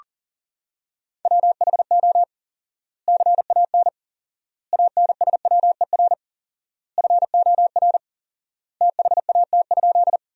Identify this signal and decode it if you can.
Morse code, heard as a keyed tone, about 40 wpm, 700 Hz. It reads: WHO CAN ANSWER FOR THAT?
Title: 4x Word-Spacing - Sentences from Top 300 Words - 40wpm